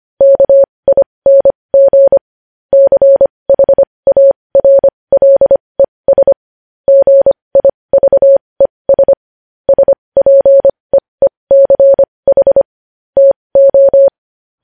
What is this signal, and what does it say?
Morse code, heard as a keyed tone, about 25 wpm, spaced as for 20 wpm, 550 Hz. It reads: KING CHARLES GIVES SPEECH TO
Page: news headlines in morse code at 20 wpm